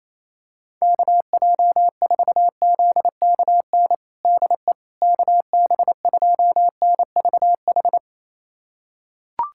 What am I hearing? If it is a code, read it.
Morse code, heard as a keyed tone, about 28 wpm, 700 Hz. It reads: KJ4ZKN DE KB2NVH
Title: USA Callsign DE Callsign - 6 Characters - 28wpm